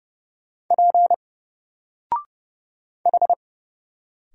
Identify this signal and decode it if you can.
Morse code, heard as a keyed tone, about 30 wpm, 700 Hz. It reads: P H
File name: Single-Letter-Number-30wpm